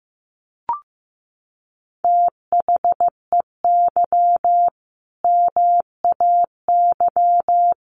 Morse code, heard as a keyed tone, about 15 wpm, 700 Hz. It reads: THEY MAY
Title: Sentences from Top 1200 Words 15wpm